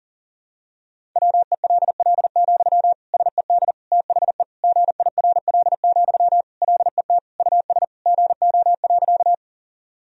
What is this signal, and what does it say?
Morse code, heard as a keyed tone, about 40 wpm, 700 Hz. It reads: WELL, SED THE GIRL, LET US GO.
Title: Sentences from Top 400 Words and Abbreviations 40wpm